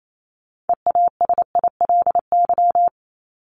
Morse code, heard as a keyed tone, about 28 wpm, 700 Hz. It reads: EASILY
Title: LY Suffix - Top 100 Words with Base Words 28wpm